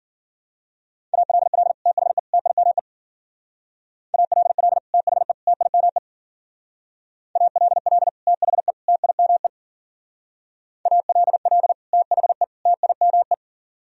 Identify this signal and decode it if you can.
Morse code, heard as a keyed tone, about 48 wpm, 700 Hz. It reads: ALL THE TIME ALL THE TIME ALL THE TIME ALL THE TIME
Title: Speed-Racing - Sets of 3 Words 40wpm